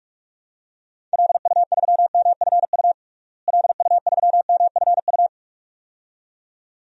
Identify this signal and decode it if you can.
Morse code, heard as a keyed tone, about 46 wpm, 700 Hz. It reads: LU3MFU LU3MFU
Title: DXCC-100 Call Signs - 6 Characters - Repeated 2x - 45wpm